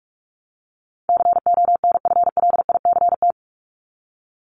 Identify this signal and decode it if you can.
Morse code, heard as a keyed tone, about 44 wpm, 700 Hz. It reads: CONFLICT
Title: Top 1001-1250 Words 45wpm